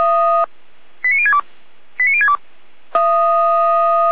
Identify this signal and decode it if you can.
Five-tone selective call (ZVEI), heard as quick runs of five tones, tones of about 70 ms. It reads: 89062 89062
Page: SDR/ZVEI